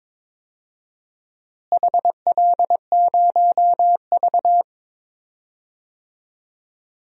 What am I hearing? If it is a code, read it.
Morse code, heard as a keyed tone, about 22 wpm, 700 Hz. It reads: HL0V
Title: DXCC-100 Call Signs - 4 Characters - Repeated 3x - 22wpm